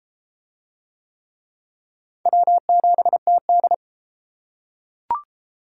Morse code, heard as a keyed tone, about 31 wpm, 700 Hz. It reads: W7TD